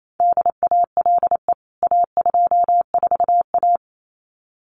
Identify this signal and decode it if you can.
Morse code, heard as a keyed tone, about 28 wpm, 700 Hz. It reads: DALE A24A
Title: CWT Exchange 28wpm